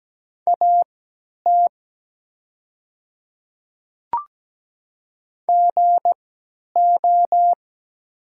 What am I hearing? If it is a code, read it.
Morse code, heard as a keyed tone, about 17 wpm, spaced as for 10 wpm, 700 Hz. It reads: AT GO
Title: 3x Character Spacing - Top 100 Words - 17wpm